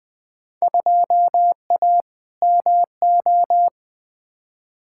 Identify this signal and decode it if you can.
Morse code, heard as a keyed tone, about 20 wpm, 700 Hz. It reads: 2A MO
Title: Field Day Exchanges Encore - 20wpm